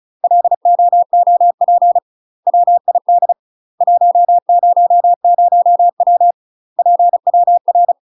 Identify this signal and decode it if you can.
Morse code, heard as a keyed tone, about 36 wpm, 700 Hz. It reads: LOOP WID 100W PWR